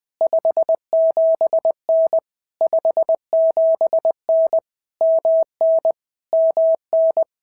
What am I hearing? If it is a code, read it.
Morse code, heard as a keyed tone, about 19 wpm, 650 Hz. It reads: 57N 57N MN MN
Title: POTA QSO Practice - 20wpm